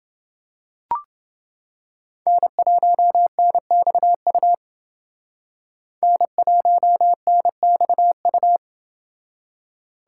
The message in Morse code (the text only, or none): N1NXU N1NXU